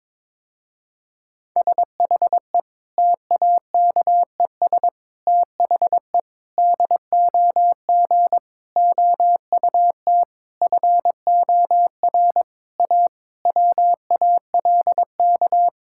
Morse code, heard as a keyed tone, about 22 wpm, 700 Hz. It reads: SHE TAKES THE DOG OUT FOR A WALK
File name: Difficult Sentences with Hints - 22wpm